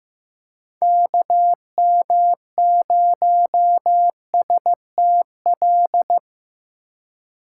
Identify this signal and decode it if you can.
Morse code, heard as a keyed tone, about 16 wpm, 700 Hz. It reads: KM0STL
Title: USA Call Signs - 4x Character Spacing - 15wpm